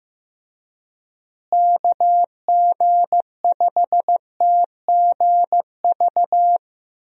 Morse code, heard as a keyed tone, about 15 wpm, 700 Hz. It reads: KG5TGV
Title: USA Call Signs - 6 Characters - Repeated 3x - 15wpm